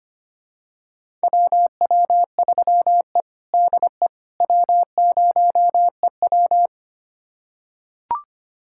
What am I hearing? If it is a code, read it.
Morse code, heard as a keyed tone, about 25 wpm, 700 Hz. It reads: WW3E DE W0EW
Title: USA Callsign DE Callsign - 4 Characters - 25wpm